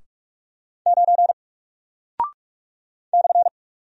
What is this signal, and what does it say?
Morse code, heard as a keyed tone, about 42 wpm, 700 Hz. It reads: 9 /
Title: Single Character (Letter:Number:Punctuation:Prosign) Alt 45wpm